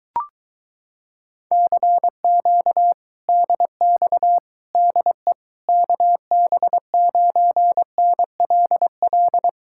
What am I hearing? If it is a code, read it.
Morse code, heard as a keyed tone, about 23 wpm, 700 Hz. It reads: CQ DX DE KB9NLL